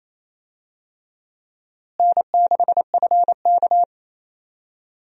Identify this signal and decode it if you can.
Morse code, heard as a keyed tone, about 28 wpm, 700 Hz. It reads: N6FK
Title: USA Call Signs - 4 Characters - Repeated 3x - 28wpm